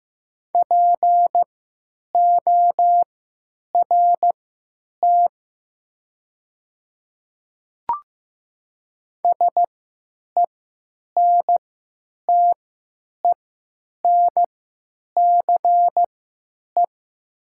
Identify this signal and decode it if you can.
Morse code, heard as a keyed tone, about 15 wpm, spaced as for 9 wpm, 700 Hz. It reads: PORT SENTENCE
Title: 3x Character Spacing - Top 300 Words - 15wpm